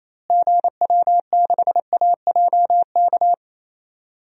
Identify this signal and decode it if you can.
Morse code, heard as a keyed tone, about 27 wpm, 700 Hz. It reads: GW6AJK
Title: DXCC-100 Call Signs - 6 Characters - Repeated 2x - 28wpm